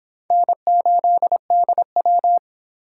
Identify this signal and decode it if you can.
Morse code, heard as a keyed tone, about 25 wpm, 700 Hz. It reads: N8DW